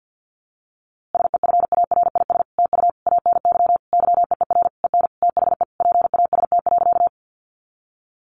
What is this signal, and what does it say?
Morse code, heard as a keyed tone, about 50 wpm, 700 Hz. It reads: SEFARIS TU ANY YEER EN THE PAST.
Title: A Sound Of Thunder 50wpm